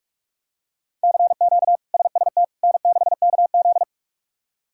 Morse code, heard as a keyed tone, about 46 wpm, 700 Hz. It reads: CQ SST N6KZ
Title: K1USN SST QSO Practice - 45wpm